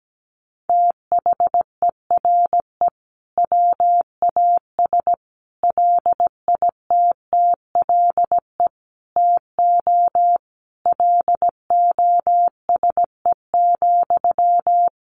Morse code, heard as a keyed tone, about 17 wpm, 700 Hz. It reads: THERE WAS LITTLE TO LOSE,
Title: Sentences from Top 1100 Words 17wpm